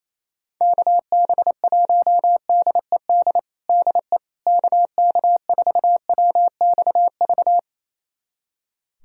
Morse code, heard as a keyed tone, about 28 wpm, 700 Hz. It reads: KB1DED DE KK4WXV
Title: USA Callsign DE Callsign - 6 Characters - 28wpm